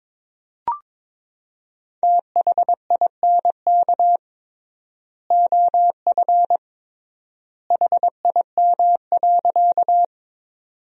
Morse code, heard as a keyed tone, about 22 wpm, 700 Hz. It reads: THINK OF HIM.
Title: 3x Word-Spacing - Sentences from Top 200 Words - 22wpm